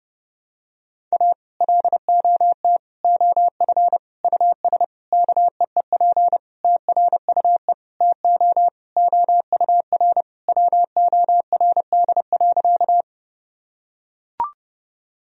Morse code, heard as a keyed tone, about 30 wpm, 700 Hz. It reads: A LOT OF US KEEP TRUE TO OUR WORD.
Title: Sentences from Top 1000 Words - Review Part 3 - 30wpm